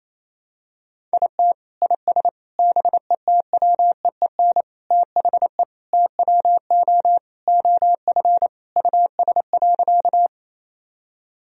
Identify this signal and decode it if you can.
Morse code, heard as a keyed tone, about 28 wpm, 700 Hz. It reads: IT IS BETWEEN THE TWO OF US.
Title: Sentences from Top 1000 Words - Review Part 1 - 28wpm